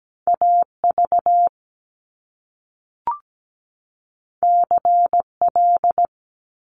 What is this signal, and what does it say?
Morse code, heard as a keyed tone, about 17 wpm, 700 Hz. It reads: AV CL
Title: Top Two Letter Combinations in Words 17wpm